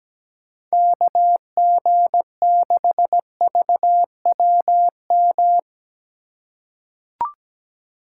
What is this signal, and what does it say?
Morse code, heard as a keyed tone, about 17 wpm, 700 Hz. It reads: KG6VWM
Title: USA Call Signs - 3x Character Spacing - 17wpm